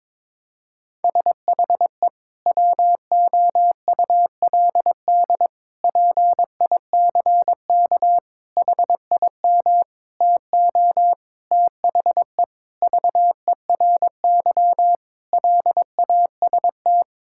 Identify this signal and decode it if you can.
Morse code, heard as a keyed tone, about 21 wpm, 700 Hz. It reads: SHE WOULD PICK HIM TO THE VERY LAST
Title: Sentences from Top 1600 Words 22wpm